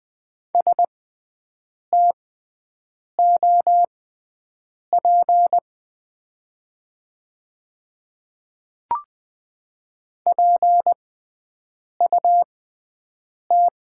Morse code, heard as a keyed tone, about 20 wpm, spaced as for 7 wpm, 700 Hz. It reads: STOP PUT
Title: 6x Character Spacing - Top 500 Words - 20wpm